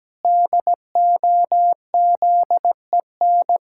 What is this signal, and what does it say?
Morse code, heard as a keyed tone, about 17 wpm, 700 Hz. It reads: DOZEN
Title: Pangram Challenge v2 - 17wpm